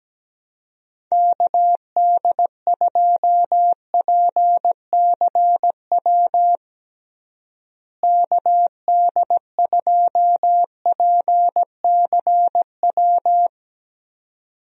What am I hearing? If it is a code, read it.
Morse code, heard as a keyed tone, about 17 wpm, 700 Hz. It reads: KD2PCW KD2PCW
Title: USA Call Signs - 6 Characters - Repeated 2x - 17wpm